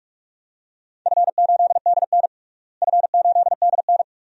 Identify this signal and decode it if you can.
Morse code, heard as a keyed tone, about 46 wpm, 700 Hz. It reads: F8DN F8DN
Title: DXCC-100 Call Signs - 4 Characters - Repeated 2x - 45wpm